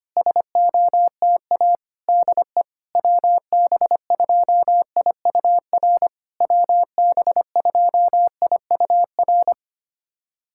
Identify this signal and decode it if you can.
Morse code, heard as a keyed tone, about 25 wpm, 700 Hz. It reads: SOTA DE WB2IUR WB2IUR
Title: SOTA QSO Practice 25wpm